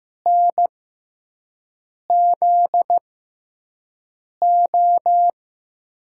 Morse code, heard as a keyed tone, about 16 wpm, spaced as for 5 wpm, 700 Hz. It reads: NZO
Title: USA Call Signs - 6 Characters - 6x Character Spacing - 15wpm